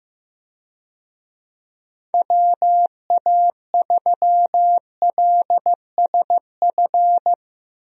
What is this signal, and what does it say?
Morse code, heard as a keyed tone, about 15 wpm, 700 Hz. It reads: WA3LSF
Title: USA Call Signs - 6 Characters - Repeated 2x - 15wpm